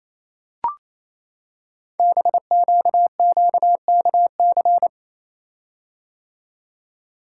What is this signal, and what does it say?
Morse code, heard as a keyed tone, about 27 wpm, 700 Hz. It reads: BQQKC
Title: Code Groups - 5 Letters - 28wpm